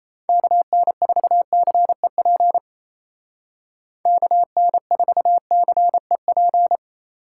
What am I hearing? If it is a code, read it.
Morse code, heard as a keyed tone, about 37 wpm, 700 Hz. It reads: KN4CEP KN4CEP